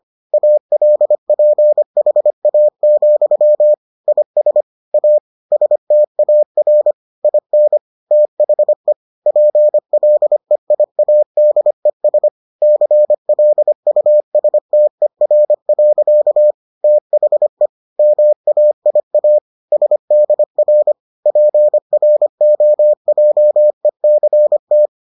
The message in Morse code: ALPHA, IS A STAR IN THE PLEIADES CLUSTER. THE MAIA SDR PROJECT

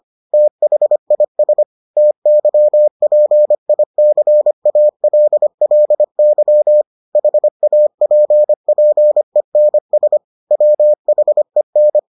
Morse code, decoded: THIS TYPICALLY HAPPENS WHEN

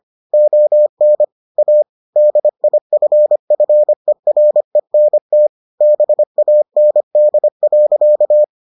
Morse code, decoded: ON A DIFFERENT BAND.

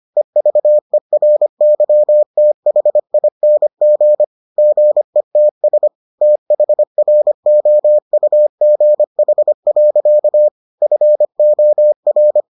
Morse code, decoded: EVERYTHING GETS THROUGH. FOR

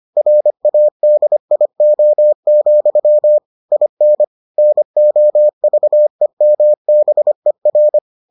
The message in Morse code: RADIO, IN NOVEMBER

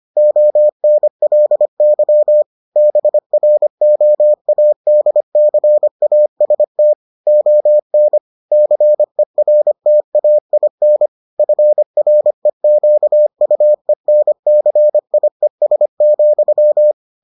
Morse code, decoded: ONLY BROADCAST ON CERTAIN FREQUENCIES,

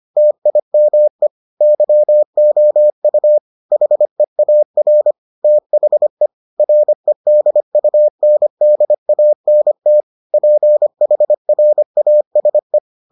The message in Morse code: TIME YOU HEAR THE REDUNDANT PHRASE

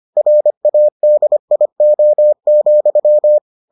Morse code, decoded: RADIO,

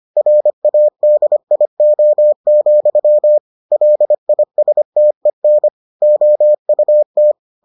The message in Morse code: RADIO, LISTEN OUT

